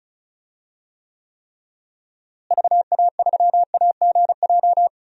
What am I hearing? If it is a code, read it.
Morse code, heard as a keyed tone, about 36 wpm, 700 Hz. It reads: VA3AGJ
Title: DXCC-100 Call Signs - 6 Characters - Repeated 3x - 35wpm